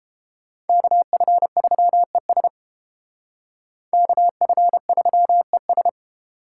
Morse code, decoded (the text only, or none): KF3ES KF3ES